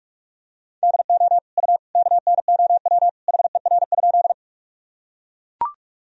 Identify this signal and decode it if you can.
Morse code, heard as a keyed tone, about 42 wpm, 700 Hz. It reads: DO U KNOW HER?
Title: Sentences from Top 100 Words and Abbreviations 45wpm